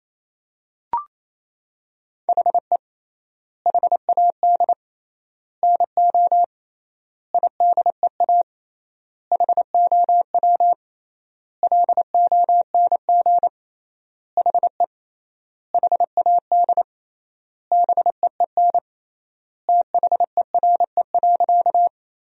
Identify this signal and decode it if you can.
Morse code, heard as a keyed tone, about 28 wpm, 700 Hz. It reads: HE HAD NO IDEA HOW LONG HE HAD BEEN THERE.
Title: 3x Word-Spacing - Sentences from Top 300 Words - 28wpm